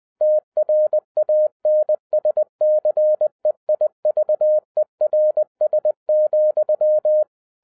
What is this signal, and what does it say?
Morse code, heard as a keyed tone, about 20 wpm, 610 Hz. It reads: TRANSCEIVERS,
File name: 2011-07-mp3 / n58 BACK FROM A BIG ACCIDENT 20wpm